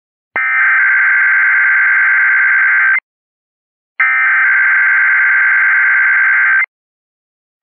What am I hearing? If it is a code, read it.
Emergency Alert System (SAME) tones (header bursts), heard as harsh buzzing bursts. It is heard from 0.4 to 6.5 s.